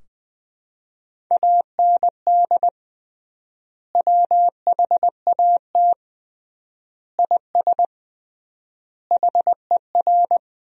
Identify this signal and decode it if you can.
Morse code, heard as a keyed tone, about 20 wpm, 700 Hz. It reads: AND WHAT IS HER
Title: 3x Word-Spacing - Sentences from Top 200 Words - 20wpm